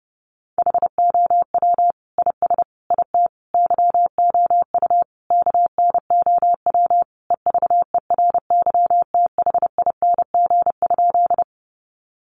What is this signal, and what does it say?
Morse code, heard as a keyed tone, about 30 wpm, 700 Hz. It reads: HOW IS IT YOU KNOW EVERYTHING?
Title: Sentences from Top 700 Words 30wpm